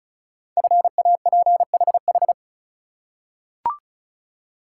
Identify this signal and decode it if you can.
Morse code, heard as a keyed tone, about 36 wpm, 690 Hz. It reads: FAPHH
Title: Code Groups - 5 Letters - 35wpm